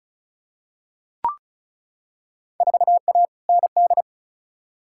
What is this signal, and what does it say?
Morse code, heard as a keyed tone, about 36 wpm, 700 Hz. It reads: 4A ND
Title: Field Day Exchanges Encore - 35wpm